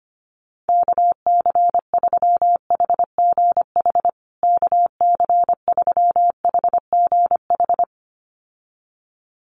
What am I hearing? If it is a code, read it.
Morse code, heard as a keyed tone, about 25 wpm, 700 Hz. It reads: KC3HGH KC3HGH
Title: USA Call Signs Repeated 2x 25wpm